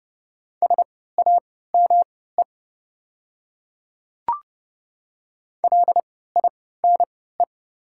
Morse code, heard as a keyed tone, about 30 wpm, spaced as for 17 wpm, 690 Hz. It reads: SAME LINE
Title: 3x Character Spacing - Top 500 Words - 30wpm